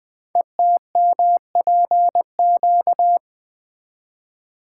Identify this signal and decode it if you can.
Morse code, heard as a keyed tone, about 20 wpm, 700 Hz. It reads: ETMPQ
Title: Code Groups - 5 Letters - 20wpm